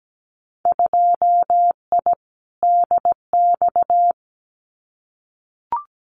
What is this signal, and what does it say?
Morse code, heard as a keyed tone, about 17 wpm, 700 Hz. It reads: 2I DX
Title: Winter Field Day Exchanges 17wpm